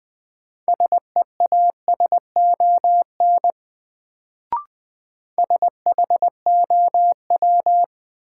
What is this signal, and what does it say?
Morse code, heard as a keyed tone, about 20 wpm, 700 Hz. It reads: SEASON SHOW
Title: Sets of 5 Words - New Word Drill - 20wpm